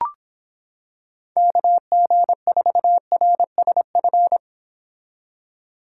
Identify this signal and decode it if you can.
Morse code, heard as a keyed tone, about 26 wpm, 700 Hz. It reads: KG4RSF